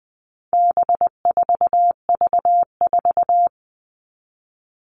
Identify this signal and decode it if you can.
Morse code, heard as a keyed tone, about 20 wpm, 700 Hz. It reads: B4V4
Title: All Cut Numbers - 4-Digit 20wpm